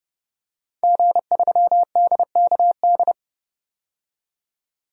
Morse code, heard as a keyed tone, about 30 wpm, 700 Hz. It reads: G3DKD
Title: DXCC-100 Call Signs - 5 Characters - 30wpm